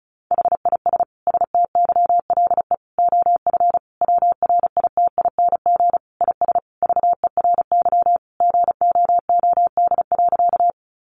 Morse code, heard as a keyed tone, about 34 wpm, 700 Hz. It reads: HIS STYLE OF WRITING IS VERY GOOD.
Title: Sentences from Top 800 Words 35wpm